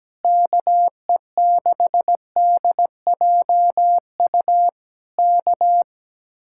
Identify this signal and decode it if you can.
Morse code, heard as a keyed tone, about 17 wpm, 700 Hz. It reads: KE6DJU K
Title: Speed-Racing Common QSO Phrases Encore - 17wpm